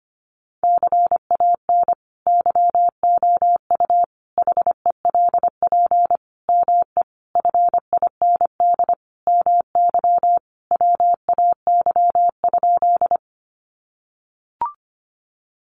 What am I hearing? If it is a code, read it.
Morse code, heard as a keyed tone, about 25 wpm, 700 Hz. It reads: CAN YOU HELP ME FIND MY WAY?
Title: Sentences from Top 200 Words 25wpm